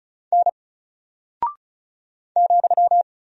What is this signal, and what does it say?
Morse code, heard as a keyed tone, about 36 wpm, 700 Hz. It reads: N ,
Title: Single Character (Letter:Number:Punctuation:Prosign) Alt 35wpm